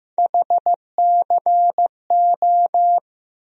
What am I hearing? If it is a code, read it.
Morse code, heard as a keyed tone, about 15 wpm, 700 Hz. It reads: HCO
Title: USA Call Sign Postfixes - 3 Letters 15wpm